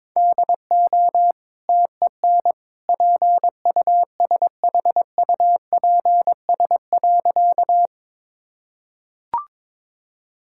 Morse code, heard as a keyed tone, about 22 wpm, 690 Hz. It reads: DO TEN PUSHUPS.
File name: practice / 22wpm / Easy Sentences with Hints - 22wpm